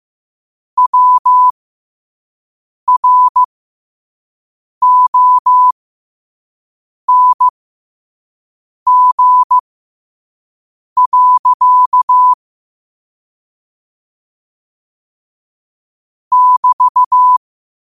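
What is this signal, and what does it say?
Morse code, heard as a keyed tone, about 15 wpm, spaced as for 5 wpm, 1000 Hz. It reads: WRONG. <BT>
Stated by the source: Morse code practice at 5 wpm.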